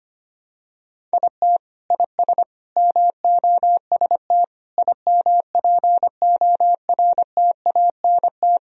Morse code, heard as a keyed tone, about 25 wpm, 700 Hz. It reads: IT IS MOST IMPORTANT
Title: Sentences from Top 1000 Words - Review Part 1 - 25wpm